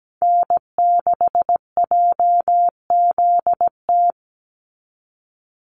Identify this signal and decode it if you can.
Morse code, heard as a keyed tone, about 17 wpm, 700 Hz. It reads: N6JZT
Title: NA SOTA Activator Callsigns 17wpm